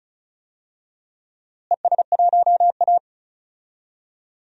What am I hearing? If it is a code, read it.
Morse code, heard as a keyed tone, about 34 wpm, 700 Hz. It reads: ES1A
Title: DXCC-100 Call Signs - 4 Characters - Repeated 3x - 35wpm